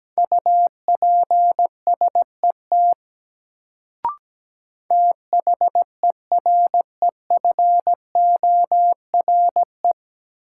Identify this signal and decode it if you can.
Morse code, heard as a keyed tone, about 17 wpm, 700 Hz. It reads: UPSET THEREFORE
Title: Sentences from Top 800 Words - New Word Drill - 17wpm